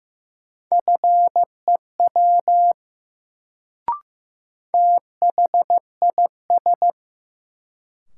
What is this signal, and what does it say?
Morse code, heard as a keyed tone, about 15 wpm, 700 Hz. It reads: FEW THIS
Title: Sets of 2 Words - New Word Drill - 15wpm